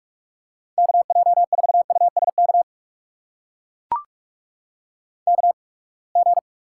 Morse code, heard as a keyed tone, about 44 wpm, spaced as for 30 wpm, 700 Hz. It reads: KJ4USK KG
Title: USA Call Signs - 6 Characters - 8x Character Spacing - 45wpm